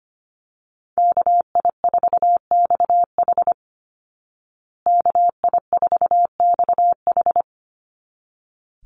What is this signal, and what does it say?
Morse code, heard as a keyed tone, about 25 wpm, 700 Hz. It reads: KI4XH KI4XH
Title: Call Signs Repeated 3x - 25wpm